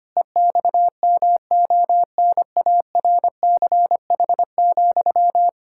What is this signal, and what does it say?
Morse code, heard as a keyed tone, about 25 wpm, 700 Hz. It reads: EXMONARCH,
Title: Pangram Challenge v2 - 25wpm